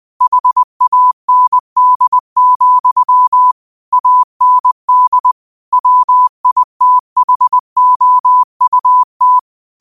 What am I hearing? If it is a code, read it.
Morse code, heard as a keyed tone, about 20 wpm, 1000 Hz. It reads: HAND, AND WITHOUT